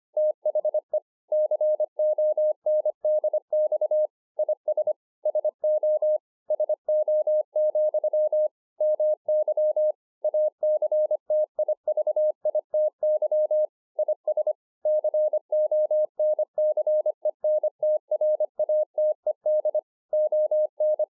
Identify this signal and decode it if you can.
Morse code, heard as a keyed tone, about 25 wpm, 610 Hz. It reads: THE CONDX IS SO SO, MY ACTIVITY IS CONCENTRATED ON